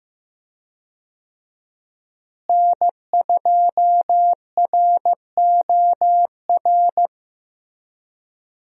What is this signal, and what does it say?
Morse code, heard as a keyed tone, about 15 wpm, 700 Hz. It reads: N2ROR
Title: USA Call Signs - 5 Characters - Repeated 2x - 15wpm